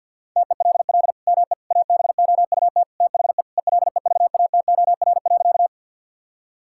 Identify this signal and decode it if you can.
Morse code, heard as a keyed tone, about 50 wpm, 700 Hz. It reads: TELL ME ABOUT THE ELEVATOR.
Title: Sentences from Top 1600 Words 50wpm